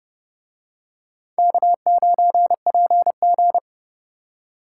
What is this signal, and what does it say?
Morse code, heard as a keyed tone, about 31 wpm, 700 Hz. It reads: K9PG